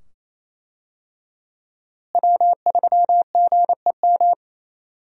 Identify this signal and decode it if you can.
Morse code, heard as a keyed tone, about 28 wpm, 700 Hz. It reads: W3GEM
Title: USA Call Signs - 5 Characters - Repeated 2x - 28wpm